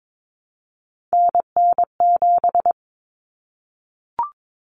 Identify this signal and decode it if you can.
Morse code, heard as a keyed tone, about 22 wpm, 700 Hz. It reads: NN7
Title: Common Cut Numbers - 3-Digit 22wpm